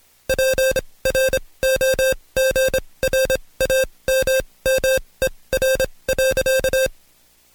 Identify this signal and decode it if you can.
Morse code, heard as a keyed tone, about 26 wpm, spaced as for 20 wpm, 530 Hz. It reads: PROGRAMMER.
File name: Code Quick / AudioDownld2 / Seg136-Stepper9 19WPM